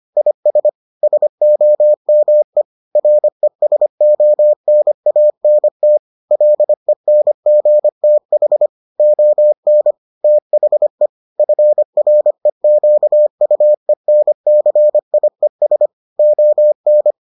Morse code, heard as a keyed tone, about 25 wpm, 600 Hz. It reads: IS SOME RESONANT LENGTH ON THE FREQUENCIES ON